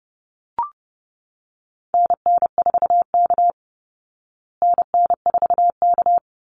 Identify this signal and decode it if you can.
Morse code, heard as a keyed tone, about 30 wpm, 700 Hz. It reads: NN4K NN4K
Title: Call Signs Repeated 2x - 30wpm